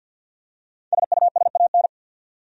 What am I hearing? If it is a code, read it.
Morse code, heard as a keyed tone, about 48 wpm, 700 Hz. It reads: SUSAN
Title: USA Common Names - Max 5 Letters - 50wpm